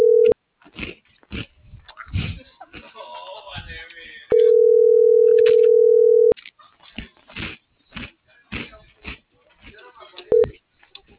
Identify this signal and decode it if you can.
Ringback tone (North America), 440 + 480 Hz.